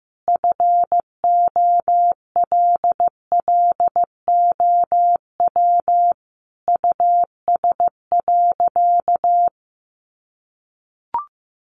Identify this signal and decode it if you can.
Morse code, heard as a keyed tone, about 15 wpm, 700 Hz. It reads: FOLLOW US.
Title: Sentences from Top 1500 Words 15wpm